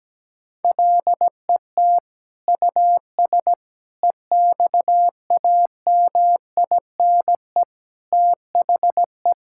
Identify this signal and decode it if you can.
Morse code, heard as a keyed tone, about 17 wpm, 700 Hz. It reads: LET US EXAMINE THE
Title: Sentences from Top 1600 Words 17wpm